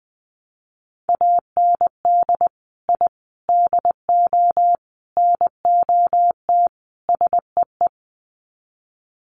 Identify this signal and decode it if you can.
Morse code, heard as a keyed tone, about 20 wpm, 700 Hz. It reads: AND I DO NOT SEE
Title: Sets of 5 Words Encore2 20wpm